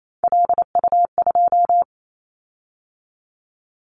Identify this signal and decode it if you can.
Morse code, heard as a keyed tone, about 28 wpm, 700 Hz. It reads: LU2
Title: Callsign Prefixes for Top 50 DXCC - 28wpm